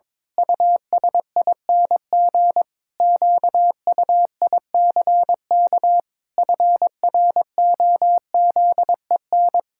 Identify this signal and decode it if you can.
Morse code, heard as a keyed tone, about 22 wpm, 700 Hz. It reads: USING QUICK FROZEN
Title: Pangram Challenge v2 - 22wpm